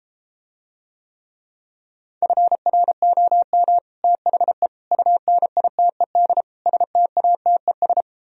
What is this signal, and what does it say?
Morse code, heard as a keyed tone, about 33 wpm, 700 Hz. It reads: FROM THE UNITED STATES